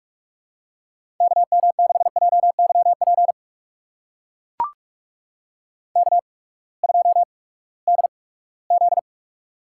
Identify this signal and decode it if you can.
Morse code, heard as a keyed tone, about 44 wpm, spaced as for 22 wpm, 700 Hz. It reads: KM6JYP K2DZ
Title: USA Call Signs - 8x Character Spacing - 45wpm